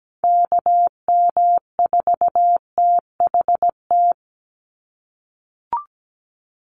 Morse code, heard as a keyed tone, about 17 wpm, 700 Hz. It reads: KM4THT
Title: USA Call Signs 17wpm